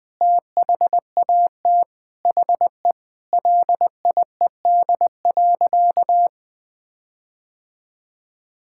Morse code, heard as a keyed tone, about 20 wpm, 700 Hz. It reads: THAT HE LIED.
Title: Sentences with 7-Word Phrases - 20wpm